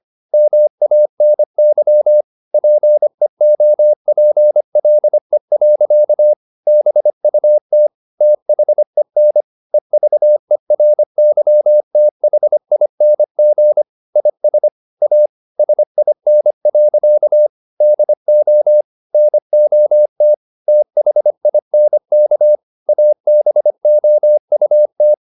Morse code, heard as a keyed tone, about 25 wpm, 600 Hz. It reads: MANY PEOPLE. BUT THEN EVERYTHING IS A SIN. DO NOT THINK ABOUT